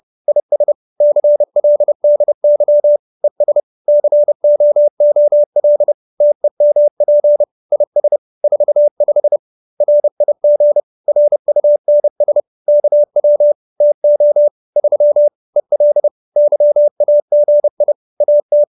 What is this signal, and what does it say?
Morse code, heard as a keyed tone, about 30 wpm, 590 Hz. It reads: IS CLDY ES COOL TEMP IS 45 RIG RUNS KW TO 3 EL YAGI AT